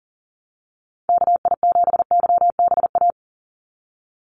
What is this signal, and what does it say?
Morse code, heard as a keyed tone, about 40 wpm, 700 Hz. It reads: KI7YBA
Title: USA Call Signs 40wpm